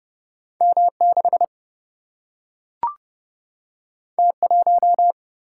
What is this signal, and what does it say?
Morse code, heard as a keyed tone, about 30 wpm, 700 Hz. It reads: M6 T1
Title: Single-Letter Single-Number 30wpm